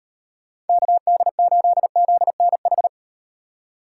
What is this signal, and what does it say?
Morse code, heard as a keyed tone, about 37 wpm, 700 Hz. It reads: KD8ZNH